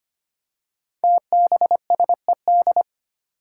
Morse code, heard as a keyed tone, about 25 wpm, 700 Hz. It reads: TBSED
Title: Five Challenging Letter-Number Combinations 25wpm